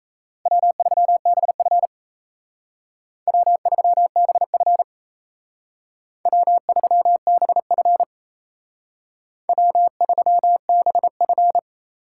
Morse code, heard as a keyed tone, about 36 wpm, 700 Hz. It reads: W3BF W3BF W3BF W3BF